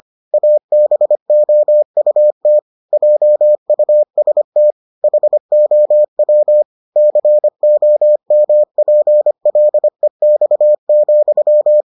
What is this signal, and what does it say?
Morse code, heard as a keyed tone, about 25 wpm, 600 Hz. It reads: ABOUT JUST HOW COMPLEX,